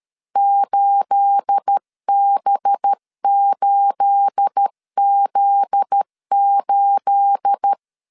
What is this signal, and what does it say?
Morse code, heard as a keyed tone, about 13 wpm, 800 Hz. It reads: 8B8Z8
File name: LEZIONI / 9 28 12wpm 2351